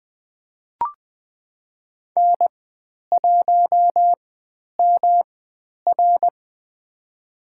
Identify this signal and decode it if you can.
Morse code, heard as a keyed tone, about 20 wpm, spaced as for 10 wpm, 700 Hz. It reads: N1MR
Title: 109 Lesson 18g - Letter m - Call Signs 20wpm